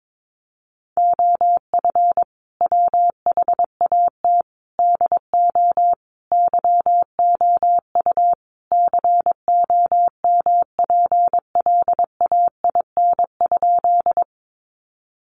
Morse code, heard as a keyed tone, about 22 wpm, 700 Hz. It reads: OF WHAT DO YOU COMPLAIN?
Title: Sentences from Top 1300 Words 22wpm